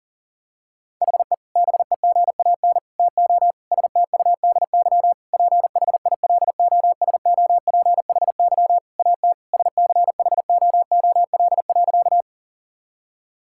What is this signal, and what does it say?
Morse code, heard as a keyed tone, about 39 wpm, 700 Hz. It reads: HE BEGAN TO STUDY PHILOSOPHY AT SCHOOL.
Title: Sentences from Top 1000 Words - Review Part 6 - 40wpm